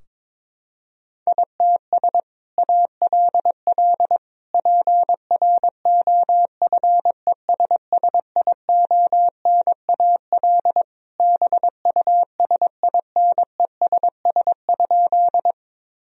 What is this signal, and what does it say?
Morse code, heard as a keyed tone, about 21 wpm, 700 Hz. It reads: ITS ALL PROFESSIONAL BUSINESS?
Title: Sentences from Top 1000 Words - Review Part 4 - 22wpm